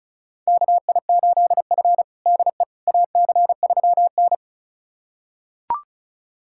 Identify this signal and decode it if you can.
Morse code, heard as a keyed tone, about 36 wpm, 700 Hz. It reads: KI8F DE AC3N
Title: USA Callsign DE Callsign - 4 Characters - 35wpm